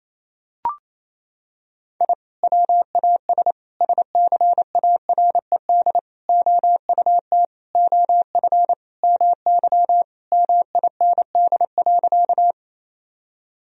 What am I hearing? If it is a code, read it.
Morse code, heard as a keyed tone, about 28 wpm, 700 Hz. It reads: I WAS SCARED OUT OF MY MIND.
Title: Sentences from Top 1000 Words - Review Part 4 - 28wpm